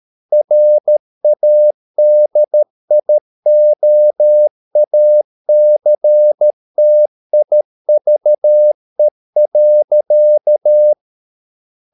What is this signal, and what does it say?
Morse code, heard as a keyed tone, about 13 wpm, 600 Hz. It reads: RADIOACTIVE.